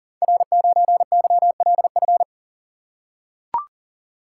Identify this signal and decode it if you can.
Morse code, heard as a keyed tone, about 39 wpm, 700 Hz. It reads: R9YLF
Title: Code Groups - 5 Letter:Numbers - 40wpm